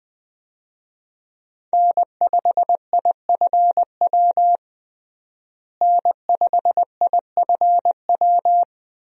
Morse code, heard as a keyed tone, about 21 wpm, 700 Hz. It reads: N5IFW N5IFW
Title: USA Call Signs - 5 Characters - Repeated 2x - 20wpm